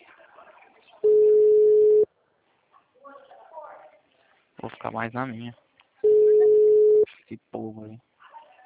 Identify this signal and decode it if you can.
Ringback tone (Europe, 425 Hz), with speech.